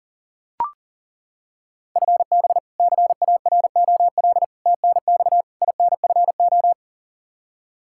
Morse code, heard as a keyed tone, about 40 wpm, 700 Hz. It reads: FB CAROL TNX INFO
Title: Practice QSO - Exchange 3 - More Info - 40wpm